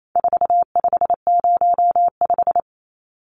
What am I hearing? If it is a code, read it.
Morse code, heard as a keyed tone, about 28 wpm, 700 Hz. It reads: 4505
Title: Numbers - 4-Digit 28wpm